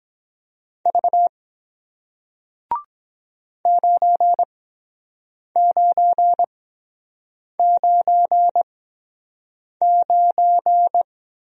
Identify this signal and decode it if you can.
Morse code, heard as a keyed tone, about 22 wpm, spaced as for 7 wpm, 700 Hz. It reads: V 9999